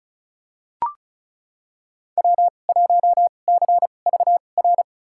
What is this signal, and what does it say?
Morse code, heard as a keyed tone, about 34 wpm, spaced as for 25 wpm, 700 Hz. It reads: W1CVR
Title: USA Call Signs - 5 Characters - 2x Character Spacing - 35wpm